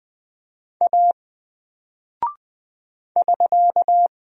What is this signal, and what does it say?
Morse code, heard as a keyed tone, about 20 wpm, 700 Hz. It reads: A <SK>
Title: Single Character (Letter:Number:Punctuation:Prosign) 20wpm